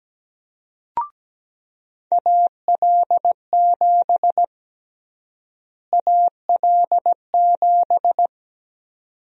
Morse code, heard as a keyed tone, about 17 wpm, 700 Hz. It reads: AL7 AL7
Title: USA Call Sign Prefixes - Repeated 2x - 17wpm